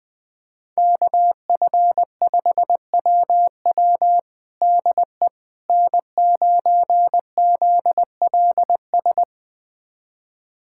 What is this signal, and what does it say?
Morse code, heard as a keyed tone, about 20 wpm, 700 Hz. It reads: KF5WW DE N9ZLS
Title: USA Callsign DE Callsign - 5 Characters - 20wpm